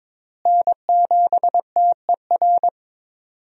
Morse code, heard as a keyed tone, about 22 wpm, 700 Hz. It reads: N7TER
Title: USA Call Signs - 5 Characters - Repeated 3x - 22wpm